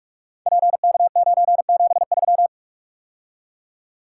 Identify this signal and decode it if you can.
Morse code, heard as a keyed tone, about 44 wpm, 700 Hz. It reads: PK973
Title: Code Groups - 5 Letter:Numbers - 45wpm